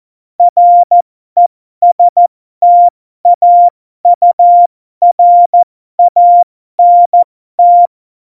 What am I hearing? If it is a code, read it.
Morse code, heard as a keyed tone, about 14 wpm, spaced as for 12 wpm, 700 Hz. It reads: RESTAURANT
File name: Morsekurs / Witze